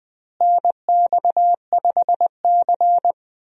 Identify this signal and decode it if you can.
Morse code, heard as a keyed tone, about 21 wpm, 700 Hz. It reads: NX5C
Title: USA Call Signs - 4 Characters - Repeated 2x - 20wpm